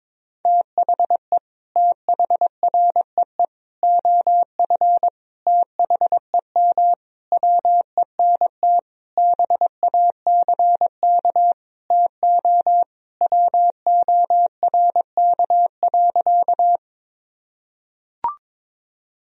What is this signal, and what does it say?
Morse code, heard as a keyed tone, about 22 wpm, 700 Hz. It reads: THE THREE OF THEM WENT BACK TO WORK.
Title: Sentences from Top 1000 Words - Review Part 1 - 22wpm